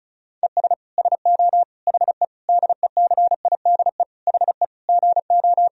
Morse code, heard as a keyed tone, about 34 wpm, 700 Hz. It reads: ES SO HE DECIDE HE GO
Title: Sentences from Top 500 Words and Abbreviations 35wpm